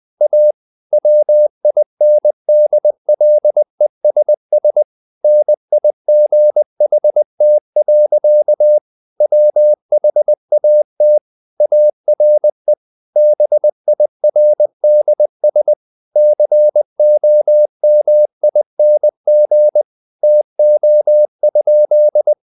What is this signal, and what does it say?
Morse code, heard as a keyed tone, about 20 wpm, 600 Hz. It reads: A WINDLESS NIGHT. WHAT ARE BIRDS COMING TO?
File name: the old man and the sea / the old man and the sea part 1 20wpm